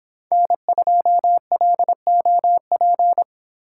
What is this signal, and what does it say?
Morse code, heard as a keyed tone, about 27 wpm, 700 Hz. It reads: N2LOP
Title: Speed Racing - USA Call Signs 22wpm